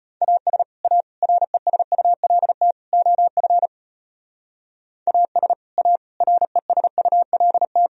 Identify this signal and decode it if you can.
Morse code, heard as a keyed tone, about 35 wpm, 700 Hz. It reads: AS A RESULT OF AS A RESULT